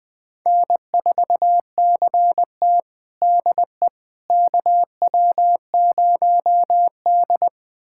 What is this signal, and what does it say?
Morse code, heard as a keyed tone, about 21 wpm, 700 Hz. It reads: N4CT DE KW0D
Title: USA Callsign DE Callsign - 4 Characters - 20wpm